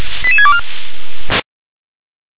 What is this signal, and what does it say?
Five-tone selective call (ZVEI), heard as quick runs of five tones, tones of about 70 ms. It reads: 88624